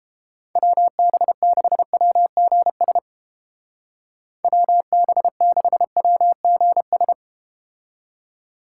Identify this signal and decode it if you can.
Morse code, heard as a keyed tone, about 32 wpm, 700 Hz. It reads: WB6WGS WB6WGS